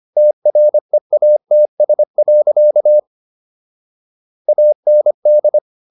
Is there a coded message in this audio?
Yes: Morse code from 0.2 to 5.6 s, heard as a keyed tone, about 25 wpm, 600 Hz. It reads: TREATS. AND